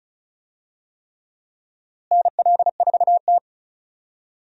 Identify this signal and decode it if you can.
Morse code, heard as a keyed tone, about 36 wpm, 700 Hz. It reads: NL4T
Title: USA Call Signs - 4 Characters - Repeated 3x - 35wpm